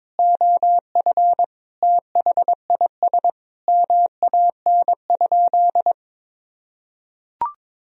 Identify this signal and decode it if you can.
Morse code, heard as a keyed tone, about 22 wpm, 700 Hz. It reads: OF THIS MAN?
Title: Sentences from Top 1000 Words - Review Part 3 - 22wpm